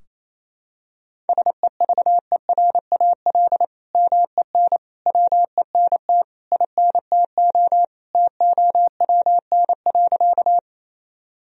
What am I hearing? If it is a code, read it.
Morse code, heard as a keyed tone, about 28 wpm, 700 Hz. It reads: SEVERAL MEN WENT INTO TOWN.
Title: Sentences from Top 1000 Words - Review Part 2 - 28wpm